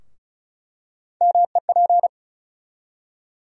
Morse code, heard as a keyed tone, about 34 wpm, 700 Hz. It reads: MEP
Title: USA Call Sign Postfixes - 3 Letters 35wpm